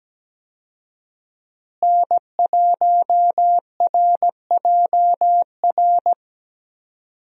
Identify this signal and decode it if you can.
Morse code, heard as a keyed tone, about 17 wpm, 700 Hz. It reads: N1RJR
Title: USA Call Signs - 5 Characters - Repeated 3x - 17wpm